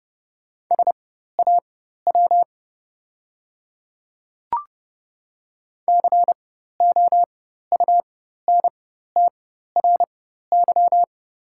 Morse code, heard as a keyed tone, about 30 wpm, spaced as for 14 wpm, 700 Hz. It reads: SAW COUNTRY
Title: 4x Character Spacing - Top 300 Words - 30wpm